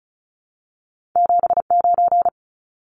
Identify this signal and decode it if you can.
Morse code, heard as a keyed tone, about 36 wpm, 700 Hz. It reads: 79
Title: Numbers - 2-Digit 35wpm